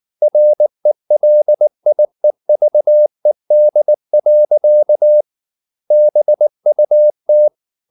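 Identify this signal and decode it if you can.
Morse code, heard as a keyed tone, about 19 wpm, 600 Hz. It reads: RELIEVED. BUT
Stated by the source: Morse code at 19 wpm from CW med Gnister, reading Ken Follett's novel Never in English.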